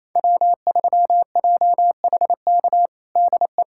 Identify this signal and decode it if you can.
Morse code, heard as a keyed tone, about 28 wpm, 700 Hz. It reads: W3JHK DE
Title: USA Callsign DE Callsign 28wpm